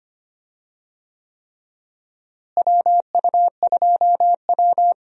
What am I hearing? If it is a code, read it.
Morse code, heard as a keyed tone, about 25 wpm, 700 Hz. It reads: WU2W
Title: USA Call Signs - 4 Characters - Repeated 3x - 25wpm